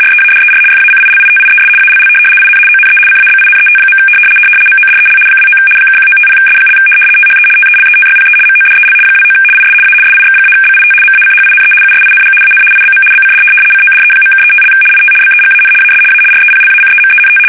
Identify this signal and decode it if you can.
RTTY, about 75 baud. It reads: READER FACILITIES ALL IN ONE CABINET. MODEL 29 (1950S) THE TELETYPE MODEL 29 IS A SIX-BIT MACHINE USING AN IBM BCD CODE.(35) IT BEGAN AS A REPLACEMENT
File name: fsk850-75B / teletype-corp